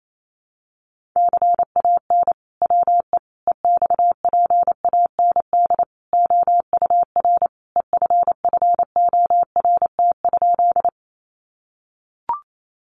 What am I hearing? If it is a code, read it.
Morse code, heard as a keyed tone, about 28 wpm, 700 Hz. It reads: CAN WE EXPAND OUR EFFORT?
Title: Sentences from Top 1400 Words 28wpm